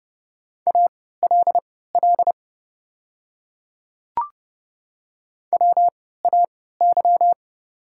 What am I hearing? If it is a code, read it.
Morse code, heard as a keyed tone, about 30 wpm, spaced as for 17 wpm, 700 Hz. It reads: ALL WAY
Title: 3x Character Spacing - Top 100 Words - 30wpm